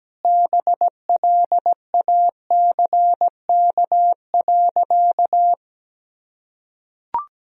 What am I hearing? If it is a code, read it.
Morse code, heard as a keyed tone, about 17 wpm, 700 Hz. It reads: BLACK.
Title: Sentences from Top 1000 Words - Review Part 2 - 17wpm